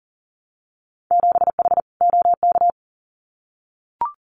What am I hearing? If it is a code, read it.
Morse code, heard as a keyed tone, about 40 wpm, 700 Hz. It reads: 7H OK
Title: Winter Field Day Exchanges Encore 40wpm